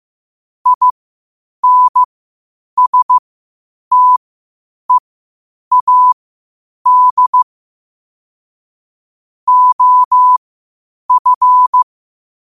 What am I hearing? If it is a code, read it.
Morse code, heard as a keyed tone, about 15 wpm, spaced as for 8 wpm, 1000 Hz. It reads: INSTEAD OF
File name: qotdfiles / 8wpm